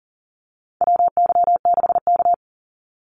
Morse code, heard as a keyed tone, about 38 wpm, 700 Hz. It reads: WY6K
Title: Call Signs - 40wpm